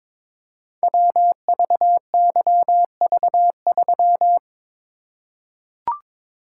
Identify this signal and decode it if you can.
Morse code, heard as a keyed tone, about 23 wpm, 700 Hz. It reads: WVYV3
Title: Code Groups - 5 Letter:Numbers - 22wpm